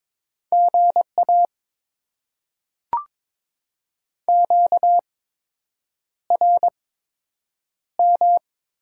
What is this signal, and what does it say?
Morse code, heard as a keyed tone, about 22 wpm, spaced as for 8 wpm, 700 Hz. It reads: GA QRM
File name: Top 90 QSO Elements - 8x Character Spacing - 22wpm